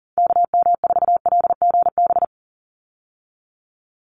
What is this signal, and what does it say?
Morse code, heard as a keyed tone, about 40 wpm, 700 Hz. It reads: KM4LGB
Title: USA Call Signs 40wpm